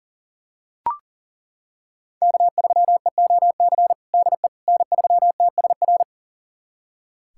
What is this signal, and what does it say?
Morse code, heard as a keyed tone, about 37 wpm, 700 Hz. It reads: K3EOC DE N3TSR
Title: USA Callsign DE Callsign - 5 Characters - 40wpm